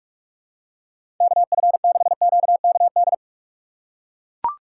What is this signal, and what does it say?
Morse code, heard as a keyed tone, about 41 wpm, 700 Hz. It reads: KF6QKD
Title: USA Call Signs - 6 Characters - 3x Character Spacing - 45wpm